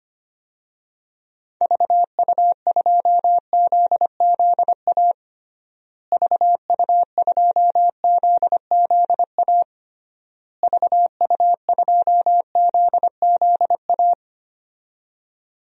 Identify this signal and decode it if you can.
Morse code, heard as a keyed tone, about 25 wpm, 700 Hz. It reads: VU2ZZA VU2ZZA VU2ZZA
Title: DXCC-100 Call Signs - 6 Characters - Repeated 3x - 25wpm